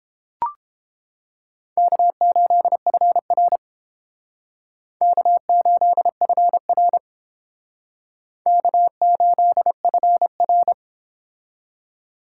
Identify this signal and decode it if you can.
Morse code, heard as a keyed tone, about 30 wpm, 700 Hz. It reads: K8FR K8FR K8FR